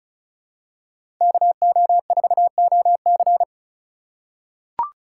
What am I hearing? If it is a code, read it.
Morse code, heard as a keyed tone, about 36 wpm, 700 Hz. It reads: KO4OC
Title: USA Call Signs - 3x Character Spacing - 35wpm